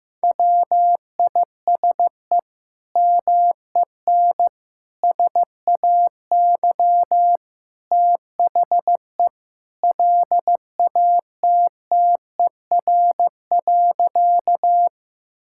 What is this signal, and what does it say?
Morse code, heard as a keyed tone, about 15 wpm, 700 Hz. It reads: WISE MEN SAY THE LATTER.
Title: Sentences from Top 1600 Words 15wpm